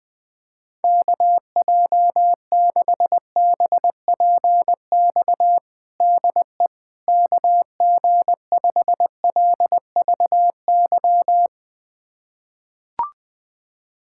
Morse code, heard as a keyed tone, about 20 wpm, 700 Hz. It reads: KJ6BPX DE KG5LVY
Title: USA Callsign DE Callsign - 6 Characters - 20wpm